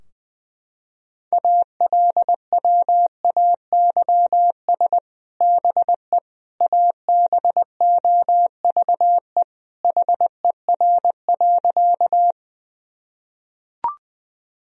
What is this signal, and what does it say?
Morse code, heard as a keyed tone, about 20 wpm, 700 Hz. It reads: ALWAYS BE ABOVE HER.
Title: Sentences from Top 1000 Words - Review Part 2 - 20wpm